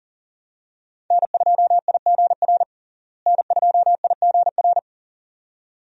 Morse code, heard as a keyed tone, about 40 wpm, 700 Hz. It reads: N2IGR N2IGR
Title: USA Call Signs - 5 Characters - Repeated 2x - 40wpm